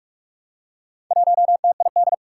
Morse code, heard as a keyed tone, about 44 wpm, 700 Hz. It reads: 1TID
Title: Four Challenging Letter-Number Combinations 45wpm